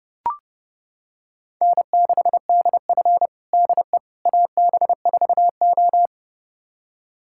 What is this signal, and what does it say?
Morse code, heard as a keyed tone, about 30 wpm, 700 Hz. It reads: N6DF DE AB4O
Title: USA Callsign DE Callsign - 4 Characters - 30wpm